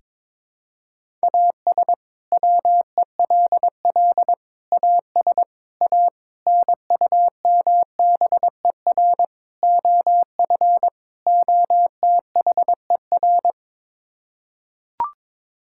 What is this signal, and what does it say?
Morse code, heard as a keyed tone, about 22 wpm, 700 Hz. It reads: AS WELL AS A NUMBER OF OTHER
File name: practice / 22wpm / Sets of 7 Words - 22wpm